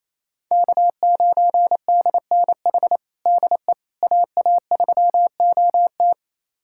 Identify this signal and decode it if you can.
Morse code, heard as a keyed tone, about 28 wpm, 700 Hz. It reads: K9DNH DE AA3OT
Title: USA Callsign DE Callsign - 5 Characters - 28wpm